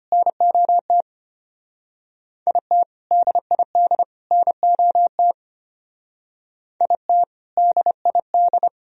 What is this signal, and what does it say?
Morse code, heard as a keyed tone, about 32 wpm, 700 Hz. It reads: NOT IT DID NOT IT DID